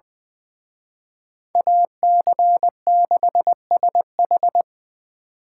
Morse code, heard as a keyed tone, about 23 wpm, 700 Hz. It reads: AC6SH
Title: Speed Racing - USA 2x2 Call Signs 20wpm